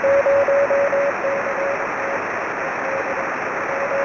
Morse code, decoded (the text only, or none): none